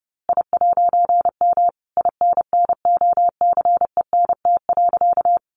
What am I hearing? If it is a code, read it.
Morse code, heard as a keyed tone, about 30 wpm, 700 Hz. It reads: I'M INNOCENT.
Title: A Sound Of Thunder 30wpm